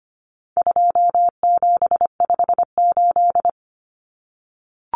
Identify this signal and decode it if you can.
Morse code, heard as a keyed tone, about 25 wpm, 700 Hz. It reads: 2758
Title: Numbers - 4-Digit 25wpm